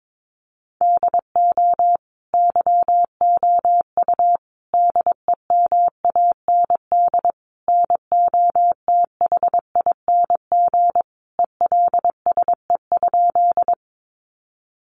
Morse code, heard as a keyed tone, about 22 wpm, 700 Hz. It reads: DO YOU DEMAND NOTHING ELSE?
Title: Sentences from Top 800 Words 22wpm